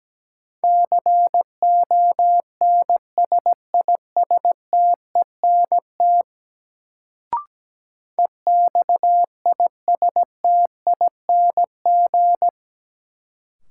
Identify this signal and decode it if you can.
Morse code, heard as a keyed tone, about 17 wpm, 700 Hz. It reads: CONSISTENT EXISTING
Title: Sentences from Top 1500 Words - New Word Drill - 17wpm